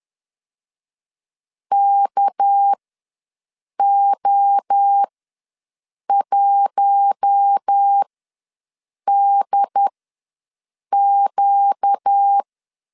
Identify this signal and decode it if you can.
Morse code, heard as a keyed tone, about 10 wpm, spaced as for 6 wpm, 800 Hz. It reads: KO1DQ
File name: LEZIONI / 6 PKX0 / 6 PK ws 10wpm 3745